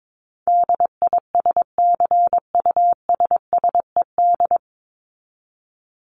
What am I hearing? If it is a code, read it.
Morse code, heard as a keyed tone, about 22 wpm, 700 Hz. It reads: DISCUSSED
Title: ED Suffix - Top 100 Words with Base Words 22wpm